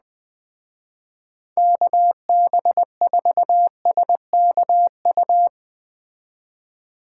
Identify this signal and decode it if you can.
Morse code, heard as a keyed tone, about 20 wpm, 700 Hz. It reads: KB4SKU
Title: Speed Racing - USA 2x3 Call Signs 17wpm